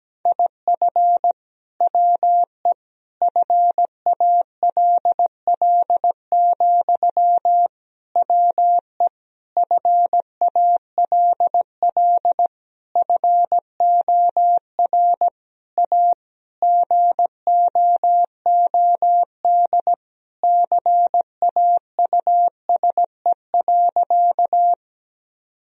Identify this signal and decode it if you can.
Morse code, heard as a keyed tone, about 17 wpm, 700 Hz. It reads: IF WE FALL, WE FALL FOR A GOOD CAUSE.
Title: Sentences from Top 1000 Words - Review Part 2 - 17wpm